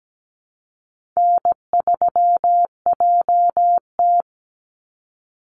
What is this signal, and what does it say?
Morse code, heard as a keyed tone, about 17 wpm, 700 Hz. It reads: N3JT
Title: Easy Call Signs Repeated 3x - 17wpm